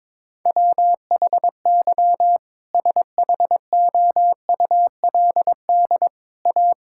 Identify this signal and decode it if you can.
Morse code, heard as a keyed tone, about 22 wpm, 700 Hz. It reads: WHY SHOULD A
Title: Sentences from Top 1600 Words 22wpm